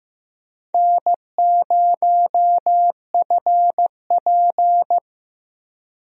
Morse code, heard as a keyed tone, about 15 wpm, 700 Hz. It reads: N0FP
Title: USA Call Signs - 4 Characters - 15wpm